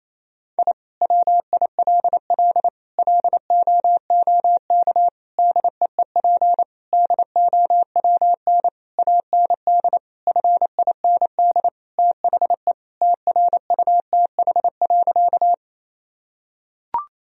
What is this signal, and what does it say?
Morse code, heard as a keyed tone, about 28 wpm, 700 Hz. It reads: I WILL LOOK DEEP DOWN AND FIND THE TRUTH.
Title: Sentences from Top 1000 Words - Review Part 3 - 28wpm